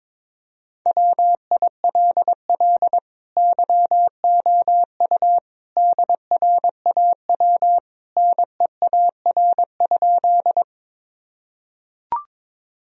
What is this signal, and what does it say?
Morse code, heard as a keyed tone, about 22 wpm, 700 Hz. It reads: WILL YOU DRAW NEAR?
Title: Sentences from Top 1000 Words - Review Part 1 - 22wpm